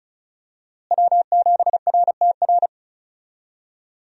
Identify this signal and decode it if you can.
Morse code, heard as a keyed tone, about 36 wpm, 700 Hz. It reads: W7RTR
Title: USA Call Signs - 5 Characters - Repeated 3x - 35wpm